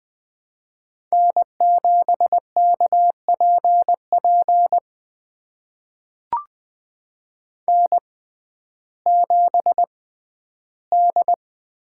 Morse code, heard as a keyed tone, about 20 wpm, spaced as for 12 wpm, 700 Hz. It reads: N7KPP N7D
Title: USA Call Signs - 5 Characters - 6x Character Spacing - 20wpm